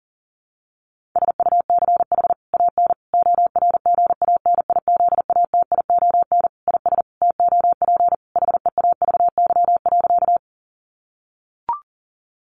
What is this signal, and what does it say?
Morse code, heard as a keyed tone, about 40 wpm, 700 Hz. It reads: SUCH AN ORGANIZATION IS TOP HEAVY.
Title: Sentences from Top 1100 Words 40wpm